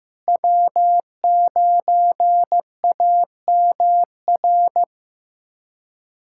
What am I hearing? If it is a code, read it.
Morse code, heard as a keyed tone, about 15 wpm, 700 Hz. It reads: W9AMR
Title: USA Callsign DE Callsign - 5 Characters - 15wpm